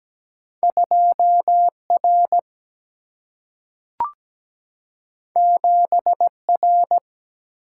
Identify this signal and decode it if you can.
Morse code, heard as a keyed tone, about 17 wpm, 700 Hz. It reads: 2R 7R
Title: Single-Letter Single-Number 17wpm